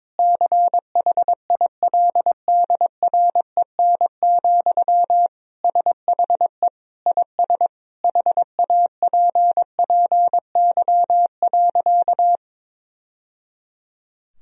Morse code, heard as a keyed tone, about 22 wpm, 700 Hz. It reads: CHILDREN, SHE IS HAPPY.
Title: Sentences from Top 1000 Words - Review Part 3 - 22wpm